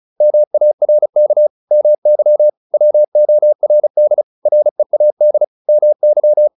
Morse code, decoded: MARK MY WORD READ MY

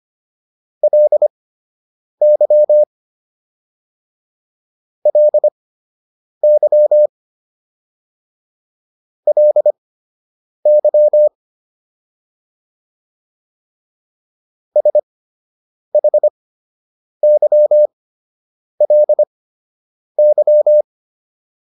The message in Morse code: LY LY LY SHYLY

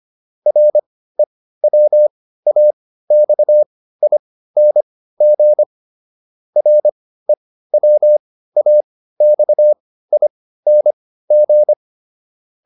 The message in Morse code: REWAXING REWAXING

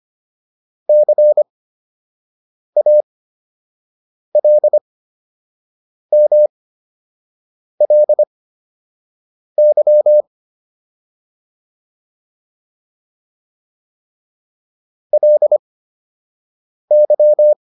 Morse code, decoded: CALMLY LY

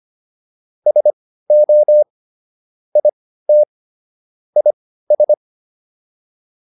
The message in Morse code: SO IT IS

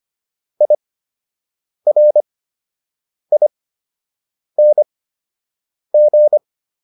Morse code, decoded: IRING

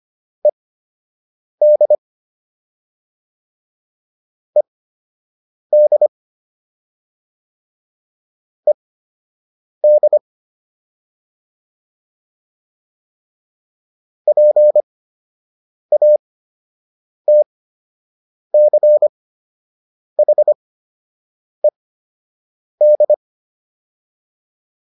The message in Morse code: ED ED ED PATCHED

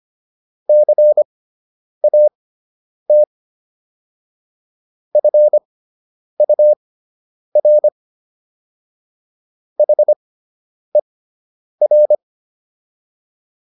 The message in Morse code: CAT FUR HER

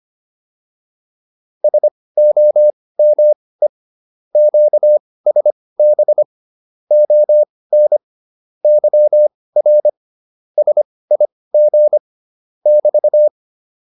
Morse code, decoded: SOME QSB ON YR SIG <BT>

